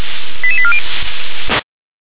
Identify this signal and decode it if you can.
Five-tone selective call (ZVEI), heard as quick runs of five tones, tones of about 70 ms. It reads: 88840
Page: /357/SDR/ZVEI/88765/